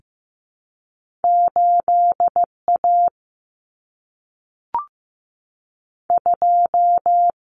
Morse code, heard as a keyed tone, about 15 wpm, 700 Hz. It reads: 8A 2
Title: Common Cut Numbers - 2-Digit 15wpm